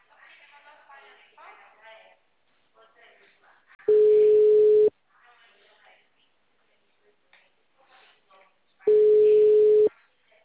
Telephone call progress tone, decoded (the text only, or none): ringback tone